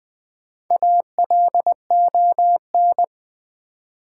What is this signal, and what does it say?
Morse code, heard as a keyed tone, about 20 wpm, 700 Hz. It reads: ALON
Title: Build A Word - 1 Letter At A Time - 5-Letter Words - 20wpm